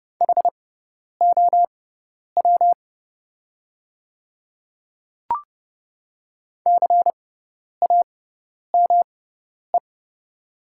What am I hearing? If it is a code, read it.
Morse code, heard as a keyed tone, about 30 wpm, spaced as for 10 wpm, 700 Hz. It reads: HOW CAME
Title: 6x Character Spacing - Top 300 Words - 30wpm